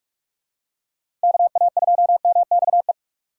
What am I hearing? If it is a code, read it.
Morse code, heard as a keyed tone, about 46 wpm, 700 Hz. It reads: KA2MXE
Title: USA Call Signs - 6 Characters - 3x Character Spacing - 45wpm